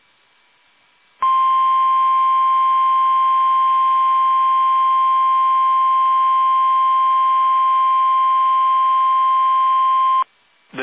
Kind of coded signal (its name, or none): Emergency Alert System (SAME) tones